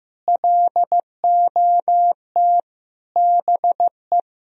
Morse code, heard as a keyed tone, about 15 wpm, 700 Hz. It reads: LOT BE
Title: Sentences from Top 1000 Words - Review Part 3 - 15wpm